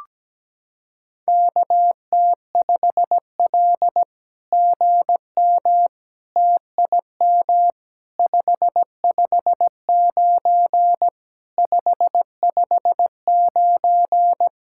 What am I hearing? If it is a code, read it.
Morse code, heard as a keyed tone, about 17 wpm, 700 Hz. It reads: KT5L GM TIM 559 559
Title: SOTA QSO Practice 17wpm